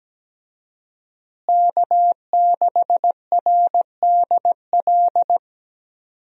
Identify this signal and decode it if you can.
Morse code, heard as a keyed tone, about 17 wpm, 700 Hz. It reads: K6RDL
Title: USA Call Signs - 5 Characters - Repeated 3x - 17wpm